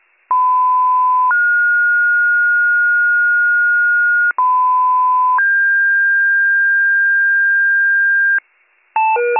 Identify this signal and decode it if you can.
Two-tone page, heard as a one-second tone, then a three-second tone, about 1010 then 1430 Hz, about 1010 then 1600 Hz, with noise. It is heard from 0.3 to 8.4 s.